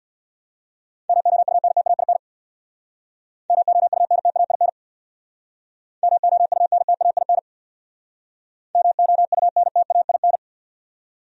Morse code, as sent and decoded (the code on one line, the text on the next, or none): -- --- ..- -. - .- .. -. / -- --- ..- -. - .- .. -. / -- --- ..- -. - .- .. -. / -- --- ..- -. - .- .. -.
MOUNTAIN MOUNTAIN MOUNTAIN MOUNTAIN